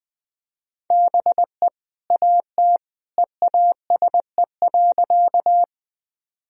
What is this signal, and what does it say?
Morse code, heard as a keyed tone, about 20 wpm, 700 Hz. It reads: BE AT EASE.
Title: Sentences from Top 1000 Words - Review Part 1 - 20wpm